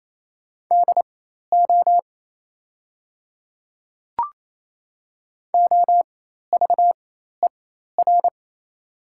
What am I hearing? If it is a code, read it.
Morse code, heard as a keyed tone, about 28 wpm, spaced as for 13 wpm, 700 Hz. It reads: DO OVER
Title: 4x Character Spacing - Top 100 Words - 28wpm